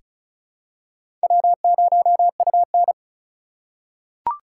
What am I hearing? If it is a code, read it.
Morse code, heard as a keyed tone, about 36 wpm, 700 Hz. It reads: W0UN
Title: USA Call Signs - 4 Characters - 2x Character Spacing - 35wpm